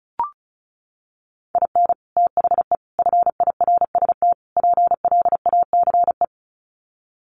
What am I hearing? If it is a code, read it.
Morse code, heard as a keyed tone, about 34 wpm, 700 Hz. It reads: IN THE FIRST PLACE
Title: Sets of 4 Words 35wpm